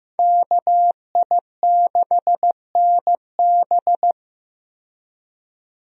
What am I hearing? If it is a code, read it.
Morse code, heard as a keyed tone, about 15 wpm, 700 Hz. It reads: KI6NB
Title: USA Call Signs - 5 Characters - 15wpm